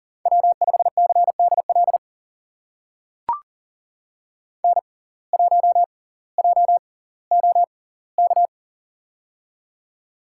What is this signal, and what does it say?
Morse code, heard as a keyed tone, about 39 wpm, spaced as for 20 wpm, 700 Hz. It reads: W5CDL N1JOK
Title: USA Call Signs - 6x Character Spacing - 40wpm